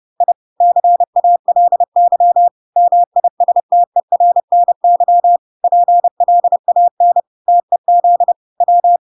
Morse code, heard as a keyed tone, about 30 wpm, 700 Hz. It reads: I CALY MISTERNY PLAN TEZ W